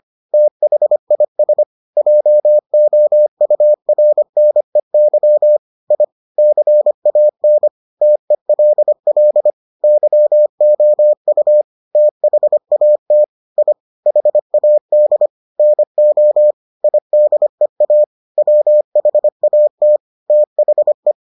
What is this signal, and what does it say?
Morse code, heard as a keyed tone, about 25 wpm, 600 Hz. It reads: THIS JOURNEY I CAN TELL YOU THAT I HAD NO IDEA WHAT THE